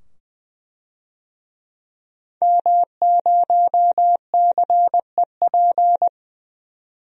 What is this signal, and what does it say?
Morse code, heard as a keyed tone, about 21 wpm, 700 Hz. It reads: M0CEP
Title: DXCC-100 Call Signs - 5 Characters - 20wpm